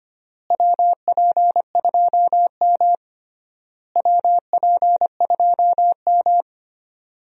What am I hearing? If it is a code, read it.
Morse code, heard as a keyed tone, about 25 wpm, 700 Hz. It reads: WP2M WP2M
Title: USA Call Signs - 4 Characters - Repeated 3x - 25wpm